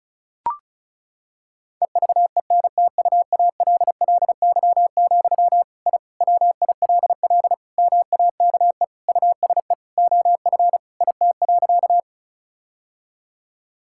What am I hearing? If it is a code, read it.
Morse code, heard as a keyed tone, about 36 wpm, 700 Hz. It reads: EVENTUALLY, I WILL MAKE USE OF IT.
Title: Sentences from Top 1000 Words - Review Part 6 - 35wpm